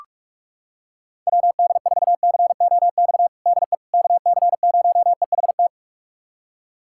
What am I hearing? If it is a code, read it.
Morse code, heard as a keyed tone, about 46 wpm, 700 Hz. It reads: WD4COX DE KC0EHT
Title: USA Callsign DE Callsign 45wpm